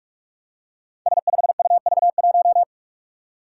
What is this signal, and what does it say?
Morse code, heard as a keyed tone, about 44 wpm, 700 Hz. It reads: S5UV1
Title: Five Challenging Letter-Number Combinations 45wpm